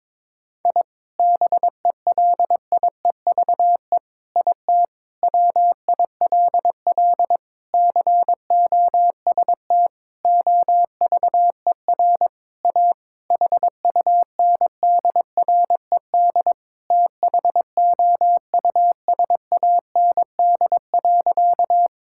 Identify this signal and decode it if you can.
Morse code, heard as a keyed tone, about 22 wpm, 700 Hz. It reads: I BELIEVE IT WILL COST OVER A HUNDRED THOUSAND.
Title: Sentences from Top 1000 Words - Review Part 5 - 22wpm